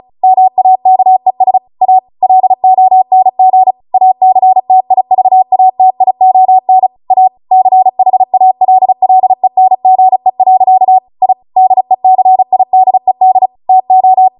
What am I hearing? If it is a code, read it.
Morse code, heard as a keyed tone, about 36 wpm, 750 Hz. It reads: MAKES A LONG ACTIVATION A CHALLENGE. I DECIDED TO
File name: Morse / 35WPM